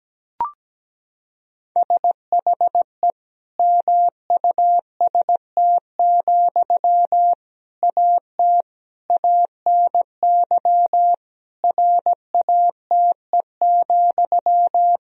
Morse code, heard as a keyed tone, about 17 wpm, 700 Hz. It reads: SHE MUST, AT ANY RATE,
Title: Sentences from Top 1000 Words - Review Part 4 - 17wpm